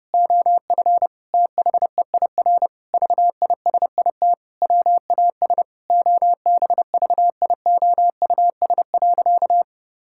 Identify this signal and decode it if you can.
Morse code, heard as a keyed tone, about 30 wpm, 700 Hz. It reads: OF THEIR VISIT WAS OBVIOUS.
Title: Sentences from Top 1600 Words 30wpm